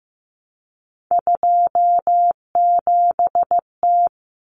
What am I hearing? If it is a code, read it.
Morse code, heard as a keyed tone, about 15 wpm, 700 Hz. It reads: 27T
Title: Common Cut Numbers - 3-Digit 15wpm